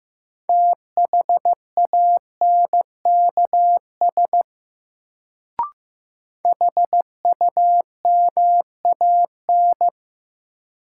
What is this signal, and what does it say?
Morse code, heard as a keyed tone, about 15 wpm, 700 Hz. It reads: THANKS HUMAN
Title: Sets of 5 Words - New Word Drill - 15wpm